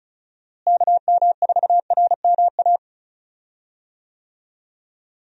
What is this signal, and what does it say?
Morse code, heard as a keyed tone, about 36 wpm, 700 Hz. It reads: KM4RMA
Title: USA Call Signs - 6 Characters - Repeated 2x - 35wpm